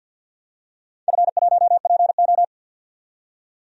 Morse code, heard as a keyed tone, about 45 wpm, 700 Hz. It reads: F1PO
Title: DXCC-100 Call Signs - 4 Characters - 50wpm